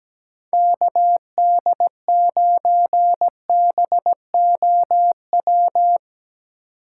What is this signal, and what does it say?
Morse code, heard as a keyed tone, about 17 wpm, 700 Hz. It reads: KD9BOW
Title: USA Call Signs - 2x Character Spacing - 17wpm